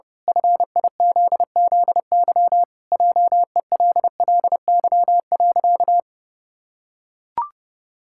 Morse code, heard as a keyed tone, about 30 wpm, 700 Hz. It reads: FIZZY JELLY.
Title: Pangram Challenge v2 - 30wpm